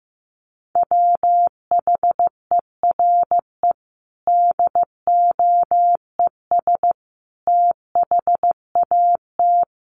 Morse code, heard as a keyed tone, about 15 wpm, 700 Hz. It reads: WHERE DOES THAT
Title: Sentences from Top 1200 Words 15wpm